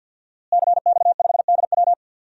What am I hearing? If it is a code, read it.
Morse code, heard as a keyed tone, about 52 wpm, 700 Hz. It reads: CX5DW
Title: DXCC-100 Call Signs - 5 Characters - Repeated 3x - 50wpm